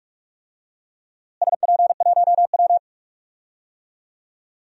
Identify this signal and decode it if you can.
Morse code, heard as a keyed tone, about 46 wpm, 700 Hz. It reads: SP1W
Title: DXCC-100 Call Signs - 4 Characters - Repeated 3x - 45wpm